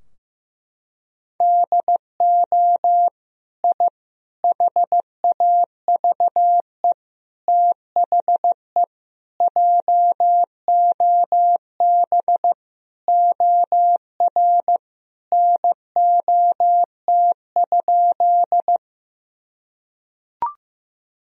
Morse code, heard as a keyed tone, about 15 wpm, 700 Hz. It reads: DO I HAVE THE JOB OR NOT?
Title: Sentences from Top 1000 Words - Review Part 3 - 15wpm